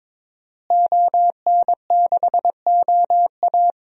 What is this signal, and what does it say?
Morse code, heard as a keyed tone, about 22 wpm, 700 Hz. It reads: ON6OA
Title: DXCC-100 Call Signs - 5 Characters - Repeated 3x - 22wpm